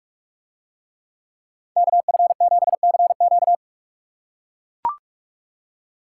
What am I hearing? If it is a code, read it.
Morse code, heard as a keyed tone, about 44 wpm, 700 Hz. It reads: KF7CQ